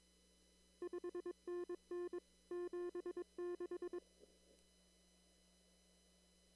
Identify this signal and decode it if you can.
Morse code, heard as a keyed tone, about 22 wpm, 350 Hz. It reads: 5NN 76
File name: CW / kt5j wpx cw-op-k5pi